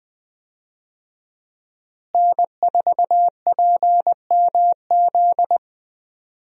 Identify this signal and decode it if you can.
Morse code, heard as a keyed tone, about 20 wpm, 700 Hz. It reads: N4PMZ
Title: USA Call Signs - 5 Characters - Repeated 3x - 20wpm